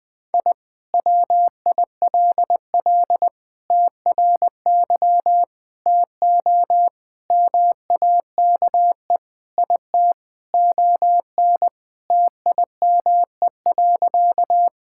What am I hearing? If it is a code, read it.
Morse code, heard as a keyed tone, about 20 wpm, 700 Hz. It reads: I WILL TRY TO MAKE IT ON TIME.
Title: Sentences with 3-Word Phrases - Spoken Beginning - 20wpm